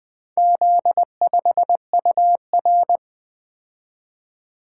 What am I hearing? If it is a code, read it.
Morse code, heard as a keyed tone, about 20 wpm, 700 Hz. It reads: Z5UR
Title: Four Letter-Number Combinations - 20wpm